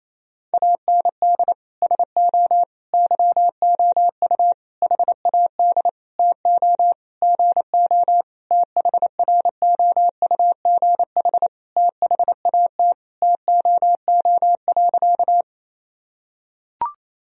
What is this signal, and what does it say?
Morse code, heard as a keyed tone, about 28 wpm, 700 Hz. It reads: AND SO YOU HAD TO GO THROUGH THAT TOO.
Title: Sentences from Top 1000 Words - Review Part 1 - 28wpm